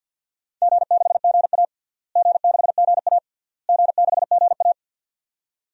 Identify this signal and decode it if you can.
Morse code, heard as a keyed tone, about 53 wpm, 700 Hz. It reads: G6GA G6GA G6GA
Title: DXCC-100 Call Signs - 4 Characters - Repeated 3x - 50wpm